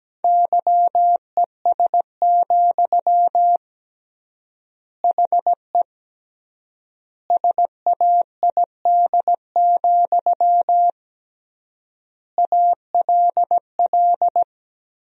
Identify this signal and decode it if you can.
Morse code, heard as a keyed tone, about 17 wpm, 700 Hz. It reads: YES, HE SAID, ALL
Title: 3x Word-Spacing - Sentences from Top 500 Words - 17wpm